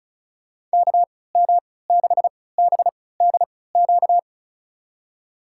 Morse code, heard as a keyed tone, about 36 wpm, spaced as for 20 wpm, 700 Hz. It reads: KM6BDQ
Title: USA Call Signs - 6 Characters - 3x Character Spacing - 35wpm